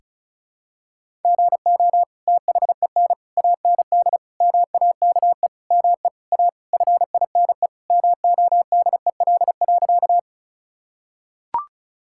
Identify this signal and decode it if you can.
Morse code, heard as a keyed tone, about 36 wpm, 700 Hz. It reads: GO THEN AND MAKE ME A FINE MODEL.
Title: Sentences from Top 1000 Words - Review Part 5 - 35wpm